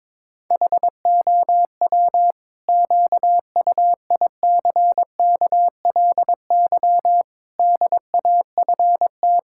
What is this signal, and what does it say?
Morse code, heard as a keyed tone, about 22 wpm, 700 Hz. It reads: HOW QUICKLY DAFT
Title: Pangram Challenge v2 - 22wpm